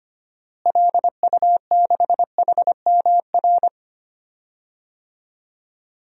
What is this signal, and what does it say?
Morse code, heard as a keyed tone, about 25 wpm, 700 Hz. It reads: LU6HMR
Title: DXCC-100 Call Signs - 6 Characters - Repeated 3x - 25wpm